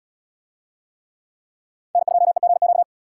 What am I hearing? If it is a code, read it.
Morse code, heard as a keyed tone, about 77 wpm, 700 Hz. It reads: N2EGQ